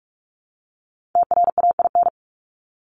Morse code, heard as a keyed tone, about 44 wpm, 700 Hz. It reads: TRAIN
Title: ING Suffix - Top 100 Words with Base Words 45wpm